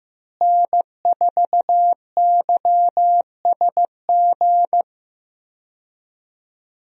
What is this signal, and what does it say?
Morse code, heard as a keyed tone, about 15 wpm, 700 Hz. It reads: N4YSG
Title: USA Call Signs - 5 Characters - Repeated 3x - 15wpm